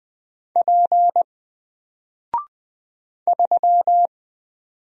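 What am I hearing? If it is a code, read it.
Morse code, heard as a keyed tone, about 20 wpm, 700 Hz. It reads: P 3
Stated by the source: Single letter and number Morse drill at 20 wpm, from N4LCD.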